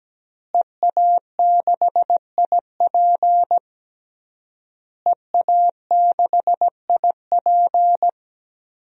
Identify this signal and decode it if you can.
Morse code, heard as a keyed tone, about 17 wpm, 700 Hz. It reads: EA6IP EA6IP
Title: DXCC-100 Call Signs - 5 Characters - Repeated 3x - 17wpm